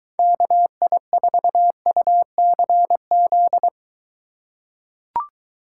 Morse code, heard as a keyed tone, about 23 wpm, 700 Hz. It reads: KI4UCZ